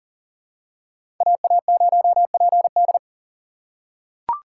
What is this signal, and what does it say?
Morse code, heard as a keyed tone, about 40 wpm, 700 Hz. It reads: AA0PD
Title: USA Call Signs - 5 Characters - 6x Character Spacing - 40wpm